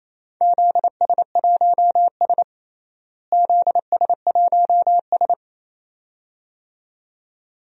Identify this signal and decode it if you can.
Morse code, heard as a keyed tone, about 28 wpm, 700 Hz. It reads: ZS1S ZS1S
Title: DXCC-100 Call Signs - 4 Characters - Repeated 2x - 28wpm